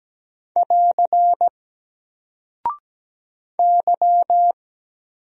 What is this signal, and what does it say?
Morse code, heard as a keyed tone, about 17 wpm, 700 Hz. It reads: <AR> Y
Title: Single Character (Letter:Number:Punctuation:Prosign) 17wpm